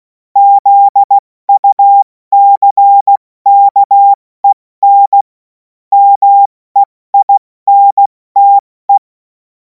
Morse code, heard as a keyed tone, about 16 wpm, spaced as for 14 wpm, 800 Hz. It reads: ZUCKEN MEINTE
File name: Morsekurs / Witze / Witz 24. Februar 2026 70 F80